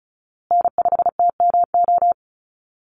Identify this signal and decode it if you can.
Morse code, heard as a keyed tone, about 34 wpm, 700 Hz. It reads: N5TMO
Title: USA Call Signs Repeated 3x 35wpm